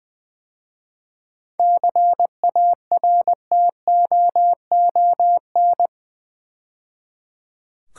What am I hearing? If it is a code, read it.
Morse code, heard as a keyed tone, about 21 wpm, 700 Hz. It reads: CARTOON
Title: 7 Letter Words - First 3 Letters Given - 20wpm